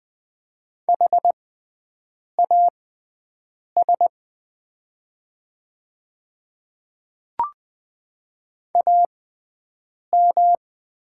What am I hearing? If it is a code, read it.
Morse code, heard as a keyed tone, about 20 wpm, spaced as for 7 wpm, 700 Hz. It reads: HAS AM
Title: 6x Character Spacing - Top 500 Words - 20wpm